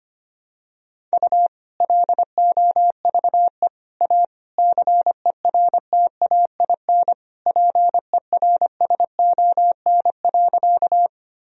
Morse code, heard as a keyed tone, about 25 wpm, 700 Hz. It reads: U LOVE A CERTAIN PERSON.
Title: Sentences from Top 500 Words and Abbreviations 25wpm